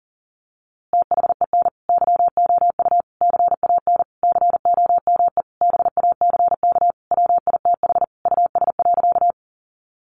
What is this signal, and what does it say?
Morse code, heard as a keyed tone, about 37 wpm, 700 Hz. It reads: THEN YOU CAN COME BACK WITH US.
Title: A Sound Of Thunder 40wpm